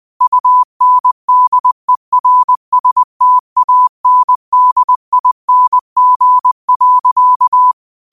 Morse code, heard as a keyed tone, about 20 wpm, 1000 Hz. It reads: UNDERSTANDING.